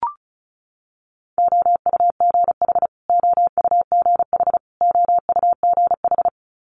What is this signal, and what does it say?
Morse code, heard as a keyed tone, about 36 wpm, 700 Hz. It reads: OUGH OUGH OUGH
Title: Multi-letter Phonemes Repeated 3x 35wpm